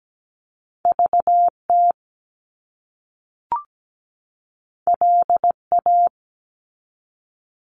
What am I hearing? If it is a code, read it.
Morse code, heard as a keyed tone, about 17 wpm, 700 Hz. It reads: VT LA
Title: US States 17wpm